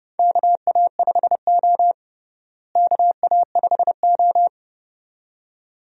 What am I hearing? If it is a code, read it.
Morse code, heard as a keyed tone, about 30 wpm, 700 Hz. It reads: KA5O KA5O
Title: USA Call Signs - 4 Characters - Repeated 2x - 30wpm